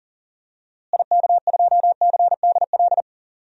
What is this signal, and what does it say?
Morse code, heard as a keyed tone, about 37 wpm, 700 Hz. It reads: IK2CDL
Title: DXCC-100 Call Signs - 6 Characters - Repeated 3x - 40wpm